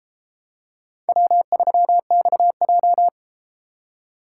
Morse code, heard as a keyed tone, about 32 wpm, 700 Hz. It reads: W3XJ